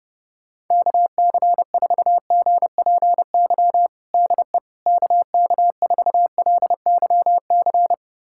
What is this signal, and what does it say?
Morse code, heard as a keyed tone, about 30 wpm, 700 Hz. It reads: KC4GPY DE KK4LYC
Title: USA Callsign DE Callsign - 6 Characters - 30wpm